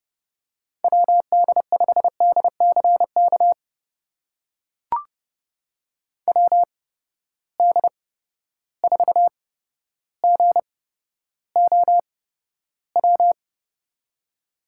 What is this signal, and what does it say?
Morse code, heard as a keyed tone, about 30 wpm, spaced as for 13 wpm, 700 Hz. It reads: WD5DCK WD4GOW
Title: USA Call Signs - 6 Characters - 8x Character Spacing - 30wpm